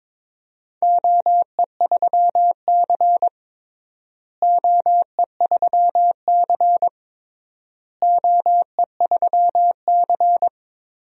Morse code, heard as a keyed tone, about 22 wpm, 700 Hz. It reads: OE3C OE3C OE3C
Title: DXCC-100 Call Signs - 4 Characters - Repeated 3x - 22wpm